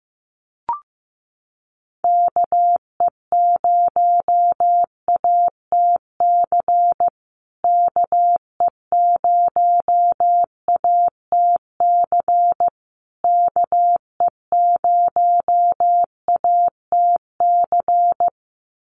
Morse code, heard as a keyed tone, about 15 wpm, 700 Hz. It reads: KE0ATC KE0ATC KE0ATC
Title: USA Call Signs Repeated 3x 15wpm